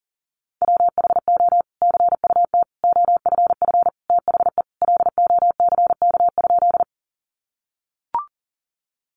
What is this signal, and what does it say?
Morse code, heard as a keyed tone, about 40 wpm, 700 Hz. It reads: WHO CUT OFF THE LOCK?
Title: Sentences from Top 1300 Words 40wpm